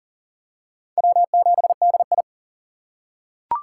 Morse code, heard as a keyed tone, about 40 wpm, 700 Hz. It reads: W7DI
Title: USA Call Signs - 4 Characters - 2x Character Spacing - 40wpm